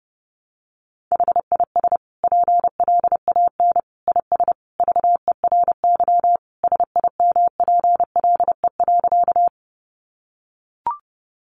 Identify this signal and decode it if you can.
Morse code, heard as a keyed tone, about 30 wpm, 700 Hz. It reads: HIS PLAN IS VERY SIMPLE.
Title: Sentences from Top 500 Words 30wpm